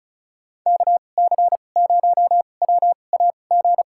Morse code, heard as a keyed tone, about 36 wpm, spaced as for 25 wpm, 700 Hz. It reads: KC0WAG
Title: USA Call Signs - 6 Characters - 2x Character Spacing - 35wpm